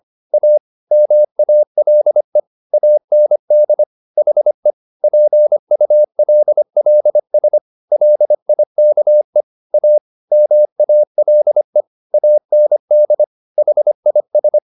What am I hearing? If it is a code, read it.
Morse code, heard as a keyed tone, about 25 wpm, 600 Hz. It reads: A MALE AND HE PULLS LIKE A MALE AND HIS
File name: the old man and the sea part 1 25wpm